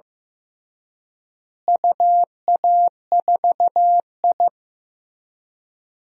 Morse code, heard as a keyed tone, about 14 wpm, 700 Hz. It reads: UA4I
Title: DXCC-100 Call Signs - 4 Characters - Repeated 3x - 15wpm